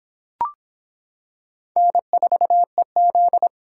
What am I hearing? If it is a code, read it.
Morse code, heard as a keyed tone, about 25 wpm, 700 Hz. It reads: N4EZ